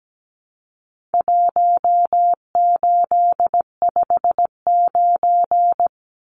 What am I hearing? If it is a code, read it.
Morse code, heard as a keyed tone, about 17 wpm, 700 Hz. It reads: 1859
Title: Numbers - 4-Digit 17wpm